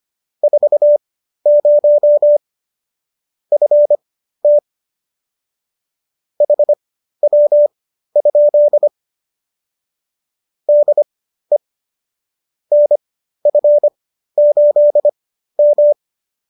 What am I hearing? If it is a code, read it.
Morse code, heard as a keyed tone, about 25 wpm, spaced as for 13 wpm, 590 Hz. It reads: 40 FT HW? DE NF8M